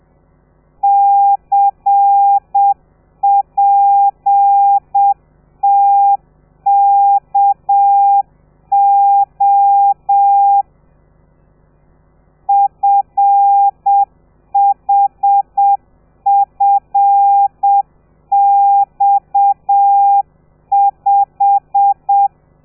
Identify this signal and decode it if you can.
Morse code, heard as a keyed tone, about 7 wpm, 800 Hz. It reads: CPTKO FHFX5
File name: telegrafia / i2tfj / Lez13 30